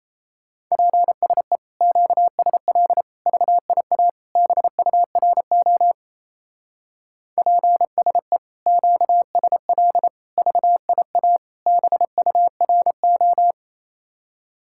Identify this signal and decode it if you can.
Morse code, heard as a keyed tone, about 36 wpm, 700 Hz. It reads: PSE QSL VIA BURO PSE QSL VIA BURO